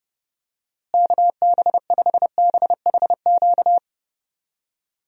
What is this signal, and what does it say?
Morse code, heard as a keyed tone, about 30 wpm, 700 Hz. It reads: KB5BHQ
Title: USA Call Signs - 6 Characters - Repeated 3x - 30wpm